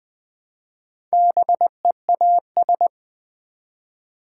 Morse code, heard as a keyed tone, about 20 wpm, 700 Hz. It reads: BEAS
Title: Build A Word - 1 Letter At A Time - 5-Letter Words - 20wpm